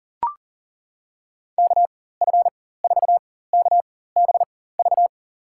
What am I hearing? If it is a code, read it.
Morse code, heard as a keyed tone, about 40 wpm, spaced as for 19 wpm, 700 Hz. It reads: KF4KBV
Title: USA Call Signs - 6 Characters - 4x Character Spacing - 40wpm